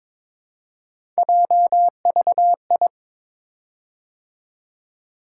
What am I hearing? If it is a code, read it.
Morse code, heard as a keyed tone, about 22 wpm, 700 Hz. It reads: JVI
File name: practice / 22wpm / Three Letter-Number Combinations - 22wpm